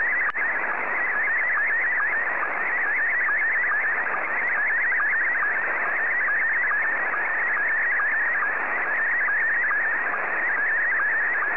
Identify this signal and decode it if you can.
SSTV (slow-scan television), Scottie 1.